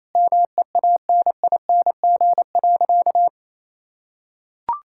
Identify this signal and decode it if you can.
Morse code, heard as a keyed tone, about 27 wpm, 700 Hz. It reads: MEANING.
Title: Sentences from Top 1000 Words - Review Part 6 - 28wpm